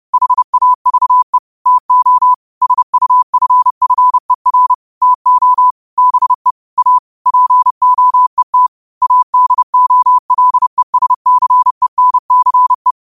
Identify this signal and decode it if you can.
Morse code, heard as a keyed tone, about 30 wpm, 1000 Hz. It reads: HAVE TO SUFFER TO BE A POET ADOLESCENCE
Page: New quotes every day in morse code at 30 Words per minute.